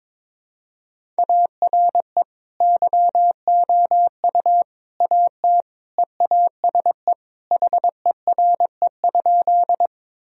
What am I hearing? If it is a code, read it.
Morse code, heard as a keyed tone, about 21 wpm, 700 Hz. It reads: ARE YOU AT EASE HERE?
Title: Sentences from Top 1000 Words - Review Part 1 - 22wpm